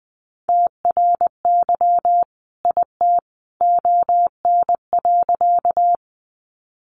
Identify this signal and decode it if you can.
Morse code, heard as a keyed tone, about 20 wpm, 700 Hz. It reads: TRY IT ON.
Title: Sentences from Top 200 Words 20wpm